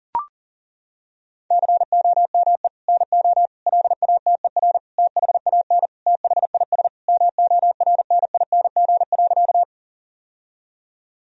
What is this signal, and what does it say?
Morse code, heard as a keyed tone, about 40 wpm, 700 Hz. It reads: COME NO LATER THAN THIS MORNING.
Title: Sentences from Top 1000 Words - Review Part 4 - 40wpm